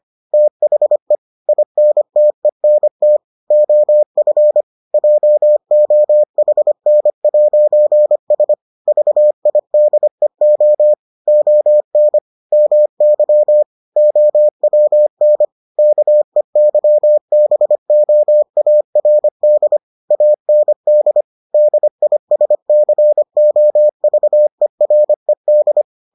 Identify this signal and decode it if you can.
Morse code, heard as a keyed tone, about 25 wpm, 600 Hz. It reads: THE INTENT OF JOHN'S VIDEO ON MY OWN KEYBOARD AND DISCOVERED